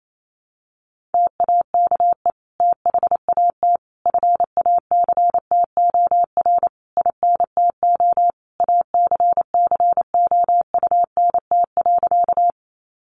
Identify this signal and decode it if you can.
Morse code, heard as a keyed tone, about 28 wpm, 700 Hz. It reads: TAKE THAT FACTOR INTO ACCOUNT.
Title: Sentences from Top 1300 Words 28wpm